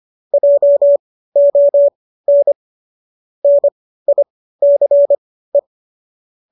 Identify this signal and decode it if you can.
Morse code, heard as a keyed tone, about 25 wpm, spaced as for 15 wpm, 590 Hz. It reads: JON NICE